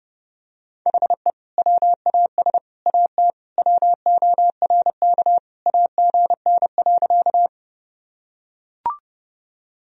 Morse code, heard as a keyed tone, about 30 wpm, 700 Hz. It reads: HE WAS AT WORK AGN.
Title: Sentences from Top 200 Words and Abbreviations 30wpm